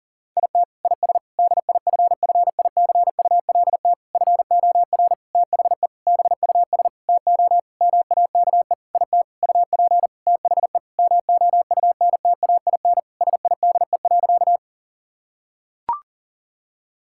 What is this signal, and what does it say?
Morse code, heard as a keyed tone, about 40 wpm, 700 Hz. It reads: IT IS DIFFICULT FOR THE BUS TO MAKE IT UP THE MOUNTAIN SIDE.
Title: Sentences from Top 1000 Words - Review Part 6 - 40wpm